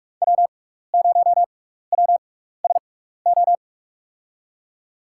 Morse code, heard as a keyed tone, about 44 wpm, spaced as for 15 wpm, 700 Hz. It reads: W0WSO
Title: USA Call Signs - 5 Characters - 6x Character Spacing - 45wpm